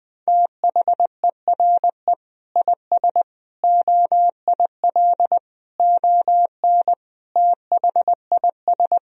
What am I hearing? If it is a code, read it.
Morse code, heard as a keyed tone, about 20 wpm, 700 Hz. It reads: THERE IS OIL ON THIS
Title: Sentences from Top 1000 Words - Review Part 4 - 20wpm